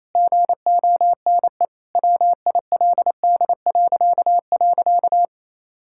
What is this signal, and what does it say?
Morse code, heard as a keyed tone, about 27 wpm, 700 Hz. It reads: GONE WILD..
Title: Pangram Challenge v2 - 28wpm